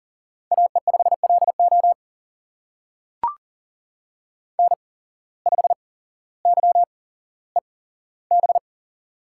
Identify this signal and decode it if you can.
Morse code, heard as a keyed tone, about 38 wpm, spaced as for 17 wpm, 700 Hz. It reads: AE5LO N5YEB
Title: USA Call Signs - 5 Characters - 8x Character Spacing - 40wpm